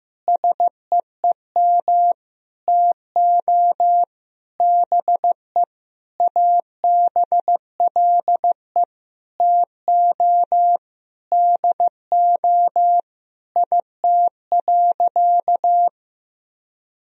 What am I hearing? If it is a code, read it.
Morse code, heard as a keyed tone, about 15 wpm, 700 Hz. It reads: SEEM TO BE ABLE TO DO IT.
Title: Sentences from Top 1000 Words - Review Part 2 - 15wpm